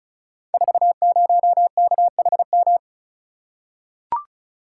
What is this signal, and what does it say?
Morse code, heard as a keyed tone, about 36 wpm, 700 Hz. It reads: 40KHM
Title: Code Groups - 5 Letter:Numbers - 35wpm